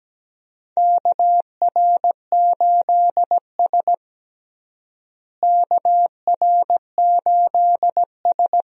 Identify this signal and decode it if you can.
Morse code, heard as a keyed tone, about 17 wpm, 700 Hz. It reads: KR8S KR8S
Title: USA Call Signs - 4 Characters - Repeated 3x - 17wpm